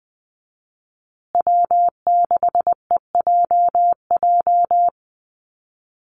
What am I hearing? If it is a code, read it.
Morse code, heard as a keyed tone, about 20 wpm, 700 Hz. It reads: W6EJJ
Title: USA Call Signs Repeated 3x 20wpm